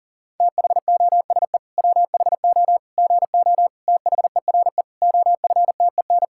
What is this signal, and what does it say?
Morse code, heard as a keyed tone, about 37 wpm, 700 Hz. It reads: THOSE WHO GO THERE OFTEN
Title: Sentences from Top 1000 Words - Review Part 5 - 40wpm